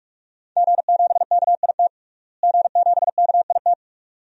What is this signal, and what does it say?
Morse code, heard as a keyed tone, about 41 wpm, 700 Hz. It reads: G7KIT G7KIT
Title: DXCC-100 Call Signs - 5 Characters - Repeated 2x - 45wpm